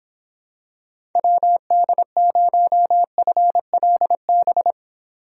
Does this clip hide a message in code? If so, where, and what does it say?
Yes: Morse code from 1.2 to 4.7 s, heard as a keyed tone, about 26 wpm, 700 Hz. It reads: WD0FLB